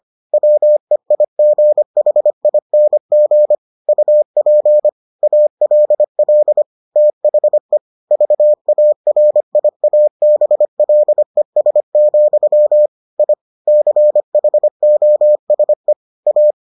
Morse code, heard as a keyed tone, about 25 wpm, 600 Hz. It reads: WEIGHING UP ALL THE VARIABLES, I CHOSE A